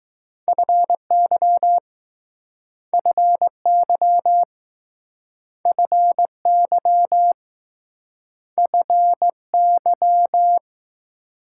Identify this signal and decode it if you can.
Morse code, heard as a keyed tone, about 20 wpm, 700 Hz. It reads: FY FY FY FY